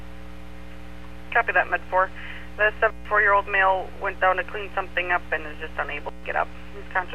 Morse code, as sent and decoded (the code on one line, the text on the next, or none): none